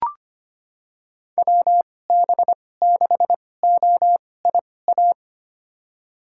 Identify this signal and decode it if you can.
Morse code, heard as a keyed tone, about 25 wpm, spaced as for 18 wpm, 700 Hz. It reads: WB6OIA
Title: USA Call Signs - 2x Character Spacing - 25wpm